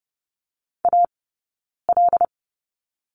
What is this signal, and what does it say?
Morse code, heard as a keyed tone, about 30 wpm, spaced as for 9 wpm, 700 Hz. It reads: AL
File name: Lesson 18d - Letter m - Two Characters 30wpm